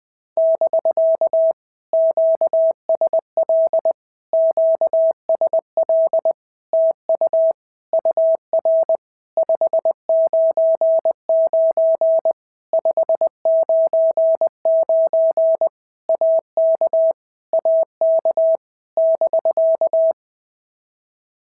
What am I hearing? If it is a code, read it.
Morse code, heard as a keyed tone, about 20 wpm, 650 Hz. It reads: <BK> QSL QSL TU UR 599 599 AK AK <BK>
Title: POTA QSO Practice - 20wpm